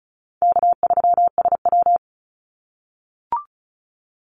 Morse code, heard as a keyed tone, about 36 wpm, 700 Hz. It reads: K3SW
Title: Call Signs - 35wpm